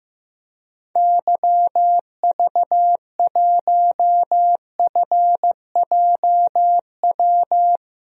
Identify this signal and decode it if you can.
Morse code, heard as a keyed tone, about 15 wpm, 700 Hz. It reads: YV1FJW
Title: DXCC-100 Call Signs - 6 Characters - 15wpm